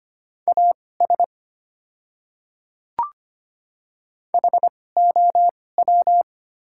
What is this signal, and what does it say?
Morse code, heard as a keyed tone, about 25 wpm, spaced as for 18 wpm, 690 Hz. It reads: AS HOW
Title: 2x Character Spacing - Top 100 Words - 25wpm